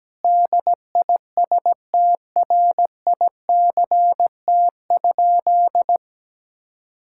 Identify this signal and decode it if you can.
Morse code, heard as a keyed tone, about 17 wpm, 700 Hz. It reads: DISTRICT?
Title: Sentences from Top 1600 Words 17wpm